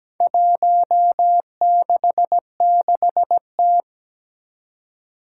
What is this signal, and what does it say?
Morse code, heard as a keyed tone, about 17 wpm, 700 Hz. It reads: 166T
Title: Four Challenging Letter-Number Combinations 17wpm